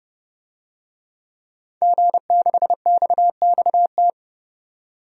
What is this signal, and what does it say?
Morse code, heard as a keyed tone, about 30 wpm, 700 Hz. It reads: G6XXT
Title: DXCC-100 Call Signs - 5 Characters - 30wpm